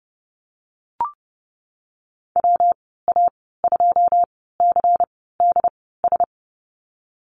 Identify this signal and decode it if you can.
Morse code, heard as a keyed tone, about 30 wpm, spaced as for 17 wpm, 700 Hz. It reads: WA2CDS
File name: Lesson 19g - Letter w - Call Signs 30wpm